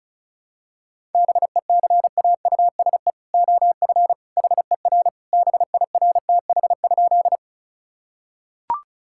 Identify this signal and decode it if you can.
Morse code, heard as a keyed tone, about 34 wpm, 700 Hz. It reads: BECAUSE OF HER BIRTH?
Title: Sentences from Top 1000 Words - Review Part 5 - 35wpm